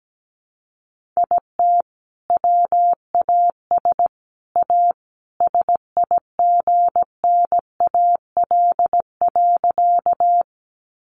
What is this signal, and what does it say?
Morse code, heard as a keyed tone, about 17 wpm, 700 Hz. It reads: IT WAS A SIGNAL.
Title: Sentences from Top 1400 Words 17wpm